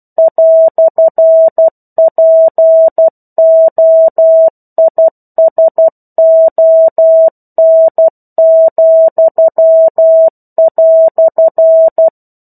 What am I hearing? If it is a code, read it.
Morse code, heard as a keyed tone, about 12 wpm, 650 Hz. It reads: "POISON,"
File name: 01-Chapter-01 12 WPM